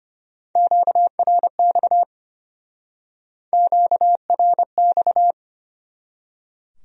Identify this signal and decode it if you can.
Morse code, heard as a keyed tone, about 35 wpm, 700 Hz. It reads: QRX QRX